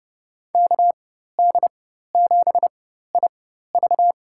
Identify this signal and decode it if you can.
Morse code, heard as a keyed tone, about 32 wpm, spaced as for 14 wpm, 700 Hz. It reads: KD7IV
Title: USA Call Signs - 4x Character Spacing - 30wpm